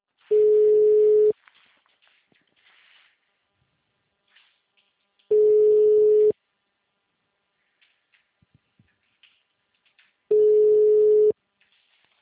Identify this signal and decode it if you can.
Ringback tone (Europe, 425 Hz).
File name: out- -2005- - - .530